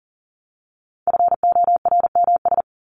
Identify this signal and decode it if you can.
Morse code, heard as a keyed tone, about 40 wpm, 700 Hz. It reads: FORMS
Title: ES:S Suffix - Top 100 Words with Base Words 40wpm